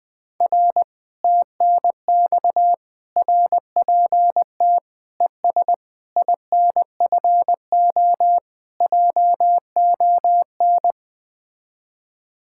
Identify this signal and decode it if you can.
Morse code, heard as a keyed tone, about 20 wpm, 700 Hz. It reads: R TNX RPT ES INFO JON
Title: Speed-Racing Common QSO Phrases - 17wpm